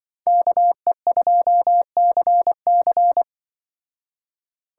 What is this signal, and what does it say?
Morse code, heard as a keyed tone, about 23 wpm, 700 Hz. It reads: KE2CC
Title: Speed Racing - USA 2x2 Call Signs 20wpm